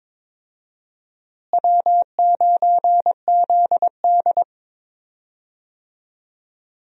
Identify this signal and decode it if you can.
Morse code, heard as a keyed tone, about 22 wpm, 700 Hz. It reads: W9ZD
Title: USA Call Signs - 4 Characters - Repeated 2x - 22wpm